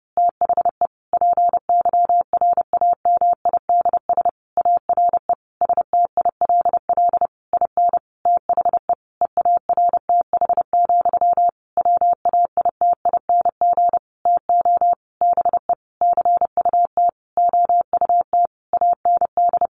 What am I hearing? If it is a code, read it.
Morse code, heard as a keyed tone, about 30 wpm, 700 Hz. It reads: THE PYRAMIDS ARE STILL IN THE EARTH, WAITING TO BE CUT OUT AND
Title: A Sound Of Thunder 30wpm